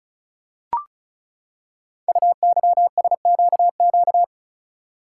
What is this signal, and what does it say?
Morse code, heard as a keyed tone, about 36 wpm, 690 Hz. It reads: UYSQQ
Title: Code Groups - 5 Letters - 35wpm